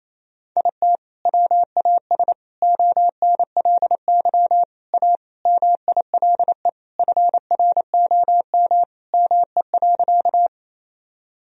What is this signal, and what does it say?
Morse code, heard as a keyed tone, about 28 wpm, 700 Hz. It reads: IT WAS ONLY A MILE FROM ME.
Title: Sentences from Top 1000 Words - Review Part 1 - 28wpm